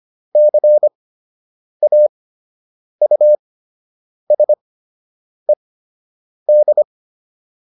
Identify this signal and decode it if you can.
Morse code, heard as a keyed tone, about 25 wpm, spaced as for 8 wpm, 600 Hz. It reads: CAUSED